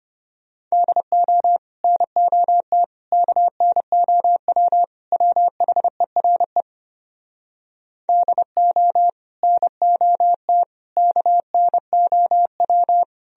Morse code, heard as a keyed tone, about 33 wpm, 700 Hz. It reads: DO NOT KNOW WHERE DO NOT KNOW